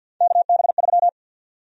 Morse code, heard as a keyed tone, about 49 wpm, 700 Hz. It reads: KB3
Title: USA Call Sign Prefixes - 50wpm